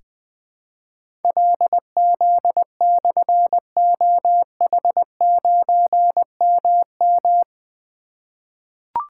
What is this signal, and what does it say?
Morse code, heard as a keyed tone, about 20 wpm, 700 Hz. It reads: LZ/OH9MM
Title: Difficult Call Signs 20wpm